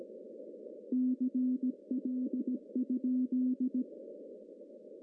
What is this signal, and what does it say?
Morse code, heard as a keyed tone, about 17 wpm, 250 Hz. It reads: CL?
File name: CW / cqww-cw-k5tr